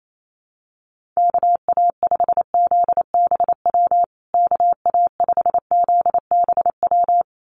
Morse code, heard as a keyed tone, about 28 wpm, 700 Hz. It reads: KA5ZBW KA5ZBW
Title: USA Call Signs Repeated 2x 28wpm